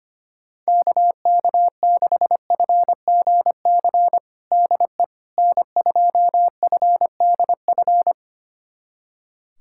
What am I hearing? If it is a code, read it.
Morse code, heard as a keyed tone, about 25 wpm, 700 Hz. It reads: KK6FGC DE N2FDF
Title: USA Callsign DE Callsign 25wpm